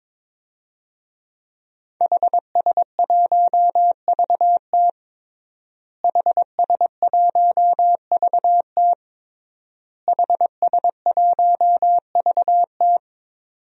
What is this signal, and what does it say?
Morse code, heard as a keyed tone, about 22 wpm, 700 Hz. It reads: HS1VT HS1VT HS1VT
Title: DXCC-100 Call Signs - 5 Characters - Repeated 3x - 22wpm